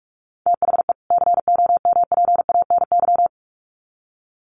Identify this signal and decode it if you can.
Morse code, heard as a keyed tone, about 44 wpm, 700 Hz. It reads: THE COMPANY
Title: Sets of 2 Words 45wpm